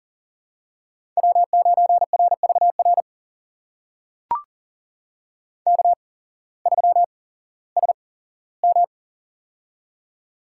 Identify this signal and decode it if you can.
Morse code, heard as a keyed tone, about 40 wpm, spaced as for 19 wpm, 700 Hz. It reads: W9RVR K3SM
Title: USA Call Signs - 8x Character Spacing - 40wpm